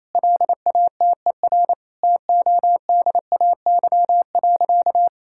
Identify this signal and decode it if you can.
Morse code, heard as a keyed tone, about 28 wpm, 700 Hz. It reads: LATER TODAY.
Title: Sentences from Top 1000 Words - Review Part 6 - 28wpm